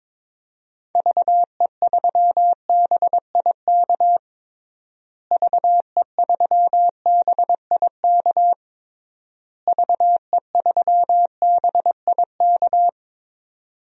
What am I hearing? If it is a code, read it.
Morse code, heard as a keyed tone, about 22 wpm, 700 Hz. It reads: VE3BIK VE3BIK VE3BIK
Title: DXCC-100 Call Signs - 6 Characters - Repeated 3x - 22wpm